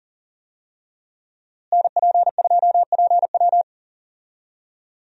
Morse code, heard as a keyed tone, about 40 wpm, 700 Hz. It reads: NP2PW
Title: USA Call Signs - 5 Characters - Repeated 3x - 40wpm